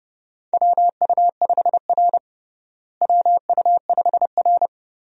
Morse code, heard as a keyed tone, about 32 wpm, 700 Hz. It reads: WU5R WU5R
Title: USA Call Signs - 4 Characters - Repeated 3x - 30wpm